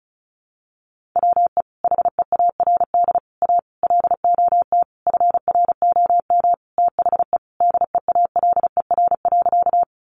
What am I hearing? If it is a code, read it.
Morse code, heard as a keyed tone, about 34 wpm, 700 Hz. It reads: WE HEARD A LOT FROM THE DEALER.
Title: Sentences from Top 1100 Words 35wpm